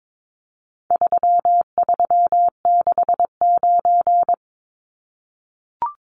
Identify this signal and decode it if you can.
Morse code, heard as a keyed tone, about 22 wpm, 700 Hz. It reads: 3369
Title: Numbers - 4-Digit 22wpm